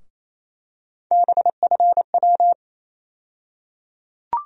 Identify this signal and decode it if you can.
Morse code, heard as a keyed tone, about 28 wpm, 700 Hz. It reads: BFW
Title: USA Call Sign Postfixes - 3 Letters 28wpm